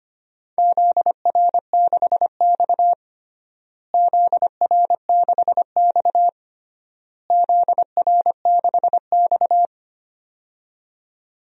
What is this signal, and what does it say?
Morse code, heard as a keyed tone, about 25 wpm, 700 Hz. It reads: ZR6X ZR6X ZR6X
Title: DXCC-100 Call Signs - 4 Characters - Repeated 3x - 25wpm